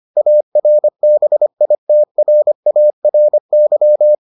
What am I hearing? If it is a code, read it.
Morse code, heard as a keyed tone, about 25 wpm, 600 Hz. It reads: ARBITRARY